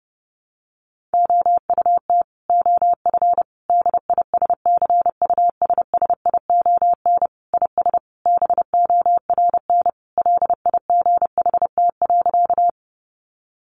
Morse code, heard as a keyed tone, about 30 wpm, 700 Hz. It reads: OUT OF DISCUSSION IS BORN LIGHT.
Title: Sentences from Top 1100 Words 30wpm